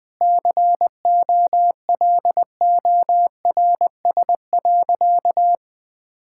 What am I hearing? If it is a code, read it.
Morse code, heard as a keyed tone, about 21 wpm, 700 Hz. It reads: COLORS.
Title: Sentences with 4-Word Phrases - Spoken Beginning - 20wpm